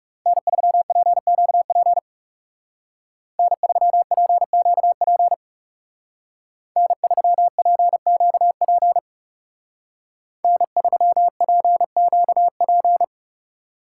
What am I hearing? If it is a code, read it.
Morse code, heard as a keyed tone, about 35 wpm, 700 Hz. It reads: N3PQP N3PQP N3PQP N3PQP
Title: Speed Racing - USA Call Signs 30wpm